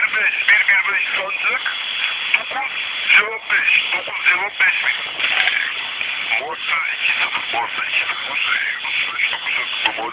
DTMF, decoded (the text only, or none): none